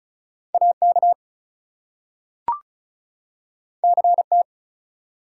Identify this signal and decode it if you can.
Morse code, heard as a keyed tone, about 36 wpm, 700 Hz. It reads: AK CT
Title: US States and Canadian Provinces 35wpm